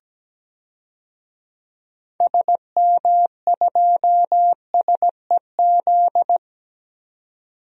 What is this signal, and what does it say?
Morse code, heard as a keyed tone, about 17 wpm, 700 Hz. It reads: SM2SEZ
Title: DXCC-100 Call Signs - 6 Characters - 17wpm